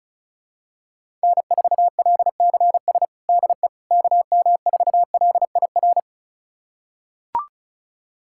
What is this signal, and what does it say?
Morse code, heard as a keyed tone, about 34 wpm, 700 Hz. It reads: N4LCS DE KM4LIR
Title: USA Callsign DE Callsign 35wpm